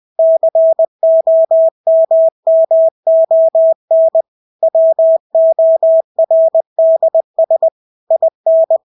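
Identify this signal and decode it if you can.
Morse code, heard as a keyed tone, about 20 wpm, 650 Hz. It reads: COMMON WORDS IN
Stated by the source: Morse practice of English words at 20 wpm.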